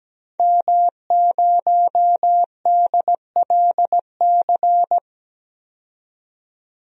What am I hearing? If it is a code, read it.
Morse code, heard as a keyed tone, about 17 wpm, 700 Hz. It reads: M0DLC
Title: DXCC-100 Call Signs - 5 Characters - 17wpm